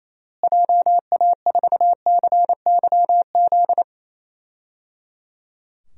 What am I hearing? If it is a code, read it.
Morse code, heard as a keyed tone, about 27 wpm, 700 Hz. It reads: JA4CYZ
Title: DXCC-100 Call Signs - 6 Characters - Repeated 3x - 28wpm